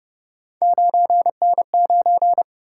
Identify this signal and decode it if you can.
Morse code, heard as a keyed tone, about 31 wpm, 700 Hz. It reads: 9N9